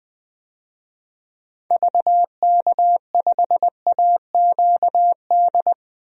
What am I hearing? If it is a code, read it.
Morse code, heard as a keyed tone, about 20 wpm, 700 Hz. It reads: VK5AQD
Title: DXCC-100 Call Signs - 6 Characters - 20wpm